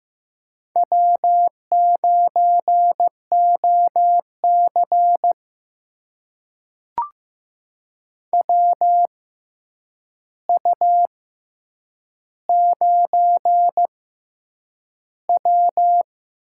Morse code, heard as a keyed tone, about 15 wpm, spaced as for 8 wpm, 700 Hz. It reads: W9OC WU9W
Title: USA Call Signs - 4 Characters - 6x Character Spacing - 15wpm